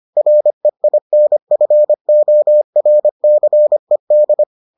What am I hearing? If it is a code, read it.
Morse code, heard as a keyed tone, about 25 wpm, 600 Hz. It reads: REINFORCED